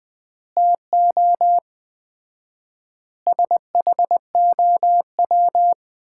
Morse code, heard as a keyed tone, about 20 wpm, 700 Hz. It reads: TO SHOW
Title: 4x Word-Spacing - Sets of 5 Words - 20wpm